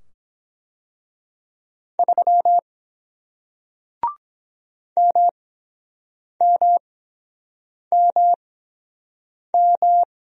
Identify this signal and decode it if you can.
Morse code, heard as a keyed tone, about 22 wpm, spaced as for 7 wpm, 700 Hz. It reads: 3 MMMM